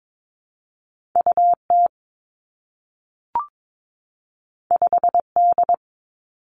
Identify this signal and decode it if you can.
Morse code, heard as a keyed tone, about 22 wpm, 700 Hz. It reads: U T 5 D
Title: Two Challenging Letter-Number Combinations 22wpm